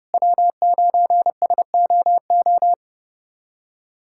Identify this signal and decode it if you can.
Morse code, heard as a keyed tone, about 30 wpm, 700 Hz. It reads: W9SOO
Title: USA Callsign DE Callsign - 5 Characters - 30wpm